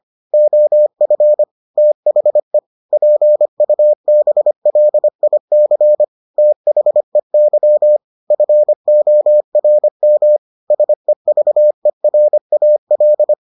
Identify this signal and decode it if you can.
Morse code, heard as a keyed tone, about 25 wpm, 600 Hz. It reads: OF THE PUBLIC THEY FORM SEVERAL